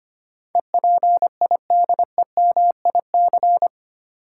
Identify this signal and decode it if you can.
Morse code, heard as a keyed tone, about 25 wpm, 700 Hz. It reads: EPIDEMIC
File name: Eight Letter Words - 25wpm